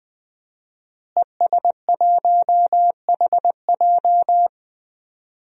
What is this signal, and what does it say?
Morse code, heard as a keyed tone, about 20 wpm, 700 Hz. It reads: ES1HJ
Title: DXCC-100 Call Signs - 5 Characters - Repeated 3x - 20wpm